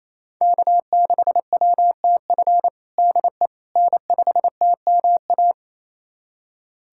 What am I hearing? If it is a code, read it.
Morse code, heard as a keyed tone, about 28 wpm, 700 Hz. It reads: K6WTF DE N5TMA
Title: USA Callsign DE Callsign - 5 Characters - 28wpm